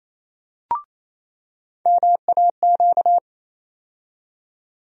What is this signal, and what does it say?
Morse code, heard as a keyed tone, about 28 wpm, 700 Hz. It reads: MAQ
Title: USA Call Sign Postfixes - 3 Letters 28wpm